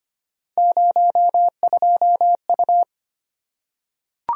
Morse code, heard as a keyed tone, about 25 wpm, 700 Hz. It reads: 02U
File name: Three Letter-Number Combinations - 25wpm